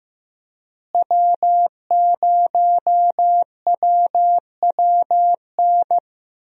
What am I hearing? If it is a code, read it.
Morse code, heard as a keyed tone, about 15 wpm, 700 Hz. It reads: W0WWN
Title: USA Call Signs - 5 Characters - Repeated 3x - 15wpm